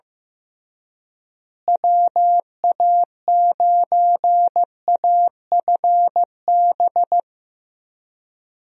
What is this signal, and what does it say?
Morse code, heard as a keyed tone, about 15 wpm, 700 Hz. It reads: WA9AFB
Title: USA Call Signs - 6 Characters - Repeated 3x - 15wpm